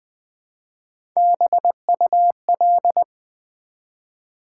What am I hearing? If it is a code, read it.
Morse code, heard as a keyed tone, about 20 wpm, 700 Hz. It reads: BUL
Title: Build A Word - 1 Letter At A Time - 4-Letter Words - 20wpm